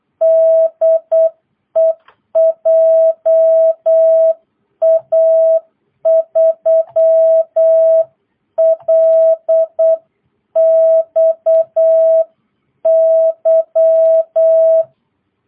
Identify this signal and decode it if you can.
Morse code, heard as a keyed tone, about 8 wpm, 660 Hz. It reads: DE JA3LXY